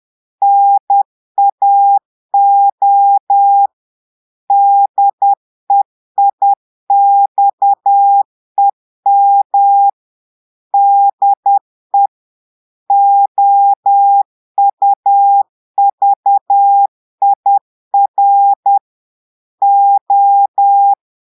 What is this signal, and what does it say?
Morse code, heard as a keyed tone, about 10 wpm, 800 Hz. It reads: NAO DEIXEM DE OUVIR O
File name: portugais texte en clair 2 vitesse 10 mots minutes